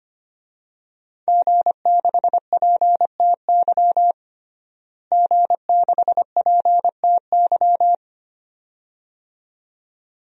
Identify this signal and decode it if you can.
Morse code, heard as a keyed tone, about 25 wpm, 700 Hz. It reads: G6PTY G6PTY
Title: DXCC-100 Call Signs - 5 Characters - Repeated 2x - 25wpm